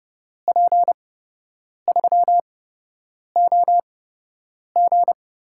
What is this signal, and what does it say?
Morse code, heard as a keyed tone, about 30 wpm, spaced as for 8 wpm, 700 Hz. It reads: P3OG
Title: USA Call Signs - 5 Characters - 8x Character Spacing - 30wpm